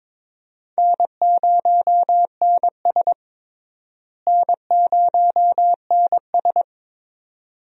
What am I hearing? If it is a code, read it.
Morse code, heard as a keyed tone, about 22 wpm, 700 Hz. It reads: N0NS N0NS
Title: USA Call Signs - 4 Characters - Repeated 2x - 22wpm